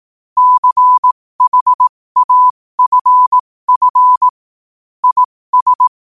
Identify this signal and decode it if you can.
Morse code, heard as a keyed tone, about 18 wpm, spaced as for 15 wpm, 1000 Hz. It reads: CHAFF IS
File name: qotdfiles / 15wpm